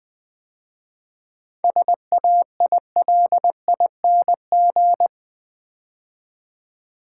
Morse code, heard as a keyed tone, about 20 wpm, 690 Hz. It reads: SAILING
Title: 7 Letter Words - First 3 Letters Given - 20wpm